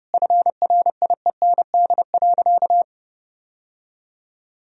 Morse code, heard as a keyed tone, about 30 wpm, 700 Hz. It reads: FRIEND.
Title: Sentences from Top 1000 Words - Review Part 4 - 30wpm